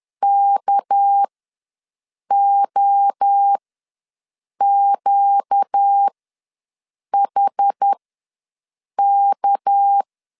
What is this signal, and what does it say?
Morse code, heard as a keyed tone, about 10 wpm, spaced as for 6 wpm, 800 Hz. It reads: KOQHK
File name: LEZIONI / 6 PKX0 / 6 PK ws 10wpm 3745